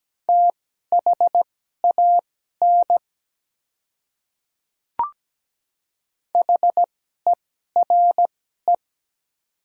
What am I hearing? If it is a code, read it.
Morse code, heard as a keyed tone, about 17 wpm, spaced as for 12 wpm, 690 Hz. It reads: THAN HERE
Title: 2x Character Spacing - Top 300 Words - 17wpm